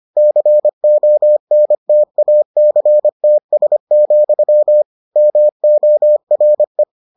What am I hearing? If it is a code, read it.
Morse code, heard as a keyed tone, about 25 wpm, 600 Hz. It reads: CONTACTS, MORE